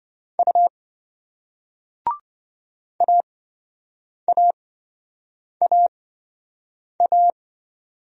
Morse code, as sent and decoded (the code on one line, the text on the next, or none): ..- / .- .- .- .-
U AAAA